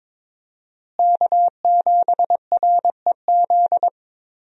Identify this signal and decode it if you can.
Morse code, heard as a keyed tone, about 22 wpm, 700 Hz. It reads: K7REZ
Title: USA Call Signs - 5 Characters - Repeated 2x - 22wpm